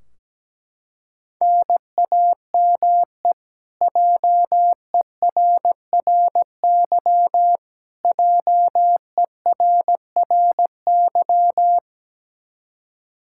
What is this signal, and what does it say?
Morse code, heard as a keyed tone, about 17 wpm, 700 Hz. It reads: NAME JERRY JERRY
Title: Practice QSO - Exchange 1 - Beginning - 17wpm